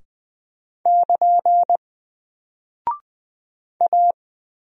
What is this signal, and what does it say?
Morse code, heard as a keyed tone, about 20 wpm, 700 Hz. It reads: <KN> A
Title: Single Character (Letter:Number:Punctuation:Prosign) Alt 20wpm